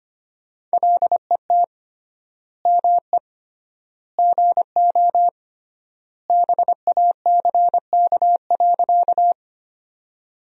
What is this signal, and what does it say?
Morse code, heard as a keyed tone, about 25 wpm, 700 Hz. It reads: LET ME GO BACK.
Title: 3x Word-Spacing - Sentences from Top 300 Words - 25wpm